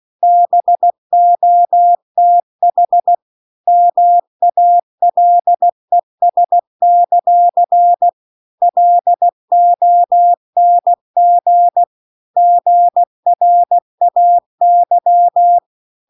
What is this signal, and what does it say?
Morse code, heard as a keyed tone, about 16 wpm, 700 Hz. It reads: BOTH MALES; LONG GRAY
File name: CW-The-Big-Twoheaded-River-2-16WPM